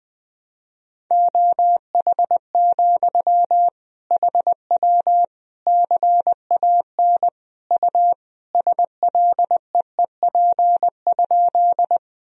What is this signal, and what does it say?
Morse code, heard as a keyed tone, about 20 wpm, 700 Hz. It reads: OH, HW CAN U SLEEP?
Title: Sentences from Top 500 Words and Abbreviations 20wpm